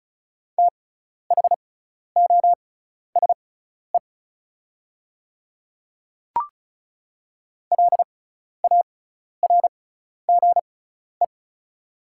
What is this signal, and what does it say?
Morse code, heard as a keyed tone, about 36 wpm, spaced as for 12 wpm, 690 Hz. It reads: THOSE LARGE
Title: 6x Character Spacing - Top 500 Words - 35wpm